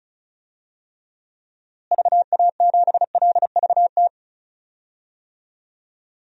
Morse code, heard as a keyed tone, about 34 wpm, 700 Hz. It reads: VA7LVT
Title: DXCC-100 Call Signs - 6 Characters - Repeated 3x - 35wpm